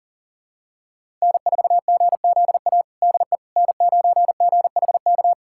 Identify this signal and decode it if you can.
Morse code, heard as a keyed tone, about 40 wpm, 700 Hz. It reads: N4GZA DE N9GHK
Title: USA Callsign DE Callsign - 5 Characters - 40wpm